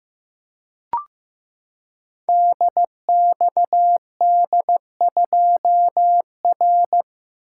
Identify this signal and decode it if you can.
Morse code, heard as a keyed tone, about 15 wpm, 700 Hz. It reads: DXD2R
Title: Code Groups - 5 Letter:Numbers - 15wpm